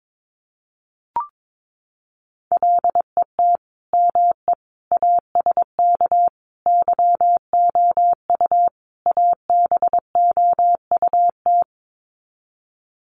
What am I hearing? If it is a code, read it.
Morse code, heard as a keyed tone, about 22 wpm, 700 Hz. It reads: LET ME ASK YOU ABOUT
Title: Sets of 5 Words 22wpm